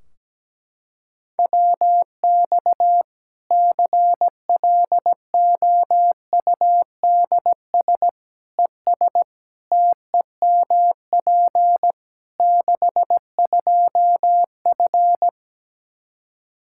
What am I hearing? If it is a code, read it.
Morse code, heard as a keyed tone, about 17 wpm, 700 Hz. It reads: WX CLOUDS ES TEMP 62F
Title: K1USN SST QSO Practice 17wpm